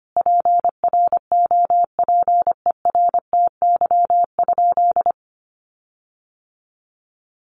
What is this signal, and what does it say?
Morse code, heard as a keyed tone, about 25 wpm, 700 Hz. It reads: PROPERTY?
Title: Sentences from Top 1500 Words 25wpm